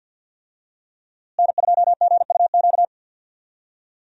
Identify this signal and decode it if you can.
Morse code, heard as a keyed tone, about 50 wpm, 700 Hz. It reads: N2GUQ
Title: USA Call Signs - 5 Characters - Repeated 2x - 50wpm